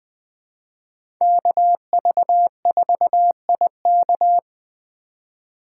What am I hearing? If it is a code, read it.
Morse code, heard as a keyed tone, about 20 wpm, 700 Hz. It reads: KV4IK
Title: USA Call Signs - 5 Characters - 3x Character Spacing - 20wpm